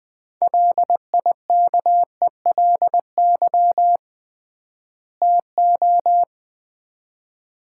Morse code, heard as a keyed tone, about 21 wpm, 700 Hz. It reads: LIKELY TO
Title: 3x Word-Spacing - Sets of 2 Words - 20wpm